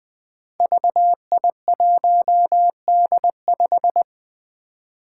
Morse code, heard as a keyed tone, about 20 wpm, 700 Hz. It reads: VI1D5
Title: Five Challenging Letter-Number Combinations 20wpm